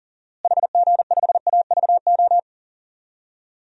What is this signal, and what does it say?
Morse code, heard as a keyed tone, about 40 wpm, 700 Hz. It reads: HG5AVO
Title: DXCC-100 Call Signs - 6 Characters - Repeated 2x - 40wpm